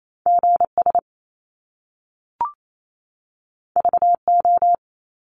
Morse code, heard as a keyed tone, about 28 wpm, 700 Hz. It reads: GS VO
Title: Top Two Letter Combinations in Words 28wpm